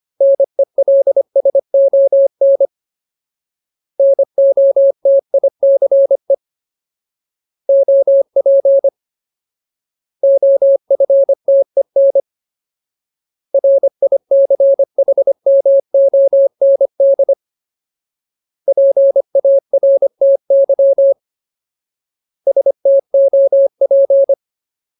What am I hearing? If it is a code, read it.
Morse code, heard as a keyed tone, about 25 wpm, 550 Hz. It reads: NELSON NOTICE OP OFTEN RICHMOND PARTY STOP
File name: Ultimate Word List / 25WPM Ultimate